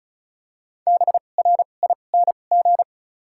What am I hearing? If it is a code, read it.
Morse code, heard as a keyed tone, about 34 wpm, spaced as for 25 wpm, 700 Hz. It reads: BRING
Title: 2x Character Spacing - Top 500 Words - 35wpm